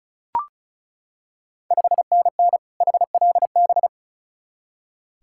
Morse code, heard as a keyed tone, about 36 wpm, 700 Hz. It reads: 5NN HLB
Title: Ontario QSO Party Exchange 35wpm